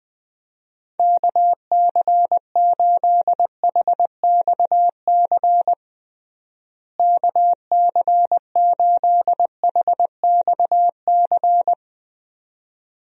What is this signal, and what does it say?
Morse code, heard as a keyed tone, about 21 wpm, 700 Hz. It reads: KC8HXC KC8HXC
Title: USA Call Signs - 6 Characters - Repeated 2x - 20wpm